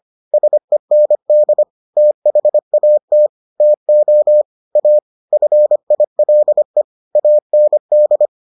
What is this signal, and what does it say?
Morse code, heard as a keyed tone, about 25 wpm, 600 Hz. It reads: SEND THAT TO A FILE AND